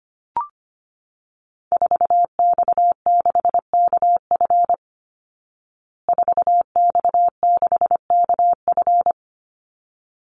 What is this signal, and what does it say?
Morse code, heard as a keyed tone, about 25 wpm, 700 Hz. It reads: 4X6KF 4X6KF
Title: Call Signs Repeated 3x - 25wpm